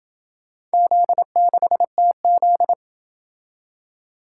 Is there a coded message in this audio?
Yes: Morse code from 0.7 to 2.7 s, heard as a keyed tone, about 27 wpm, 700 Hz. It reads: Z6TZ